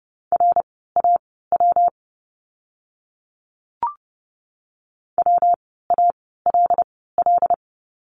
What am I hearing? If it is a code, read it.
Morse code, heard as a keyed tone, about 30 wpm, spaced as for 17 wpm, 700 Hz. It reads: RAW WALL
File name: course / 30:10wpm / 115 Lesson 19f - Letter w - Words 30wpm